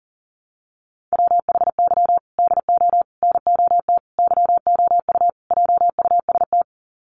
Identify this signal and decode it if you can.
Morse code, heard as a keyed tone, about 40 wpm, 700 Hz. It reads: WHY DO NOT YOU JUST
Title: Sets of 5 Words 40wpm